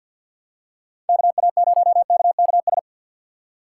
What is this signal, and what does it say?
Morse code, heard as a keyed tone, about 47 wpm, 700 Hz. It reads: KA0KKS
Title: USA Call Signs - 4x Character Spacing - 50wpm